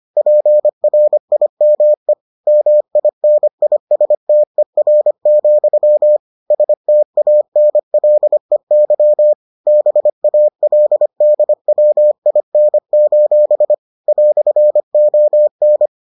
Morse code, heard as a keyed tone, about 25 wpm, 600 Hz. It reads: PRIME MINISTER, STANLEY BALDWIN: "ON